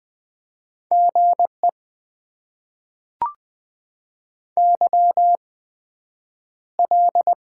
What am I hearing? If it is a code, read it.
Morse code, heard as a keyed tone, about 20 wpm, spaced as for 9 wpm, 700 Hz. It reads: GE YL
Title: Top 90 QSO Elements - 8x Character Spacing - 20wpm